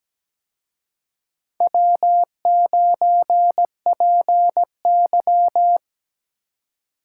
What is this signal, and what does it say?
Morse code, heard as a keyed tone, about 20 wpm, 700 Hz. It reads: W9PY
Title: Speed Racing - USA 1x2 Call Signs 17wpm